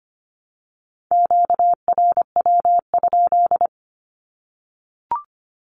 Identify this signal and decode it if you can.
Morse code, heard as a keyed tone, about 25 wpm, 700 Hz. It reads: QRW?
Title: Q-Codes Repeated 3x 25wpm